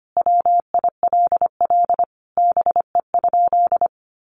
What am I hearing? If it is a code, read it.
Morse code, heard as a keyed tone, about 25 wpm, 700 Hz. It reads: WILL BE?
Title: Sentences from Top 1500 Words 25wpm